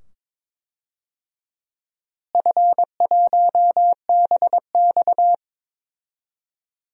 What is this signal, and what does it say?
Morse code, heard as a keyed tone, about 22 wpm, 700 Hz. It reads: F1BX
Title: DXCC-100 Call Signs - 4 Characters - Repeated 2x - 22wpm